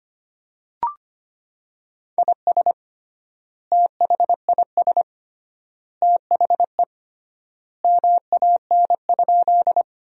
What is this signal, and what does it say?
Morse code, heard as a keyed tone, about 25 wpm, 700 Hz. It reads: IS THIS THE MAN?
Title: 3x Word-Spacing - Sentences from Top 200 Words - 25wpm